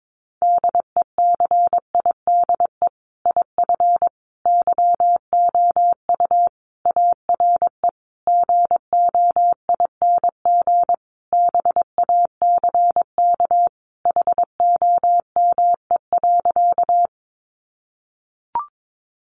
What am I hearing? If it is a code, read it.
Morse code, heard as a keyed tone, about 22 wpm, 700 Hz. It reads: DECIDE IF YOU ARE GOING BACK HOME.
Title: Sentences from Top 500 Words 22wpm